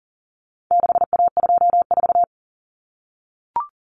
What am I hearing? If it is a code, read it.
Morse code, heard as a keyed tone, about 40 wpm, 700 Hz. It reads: 6A24
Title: Common Cut Numbers - 4-Digit 40wpm